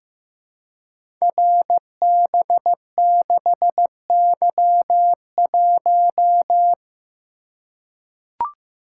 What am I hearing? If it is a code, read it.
Morse code, heard as a keyed tone, about 15 wpm, 700 Hz. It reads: RB6Y1
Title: Code Groups - 5 Letter:Numbers - 15wpm